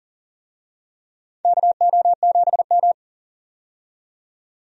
Morse code, heard as a keyed tone, about 39 wpm, 690 Hz. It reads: KO7M
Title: USA Call Signs - 4 Characters - 40wpm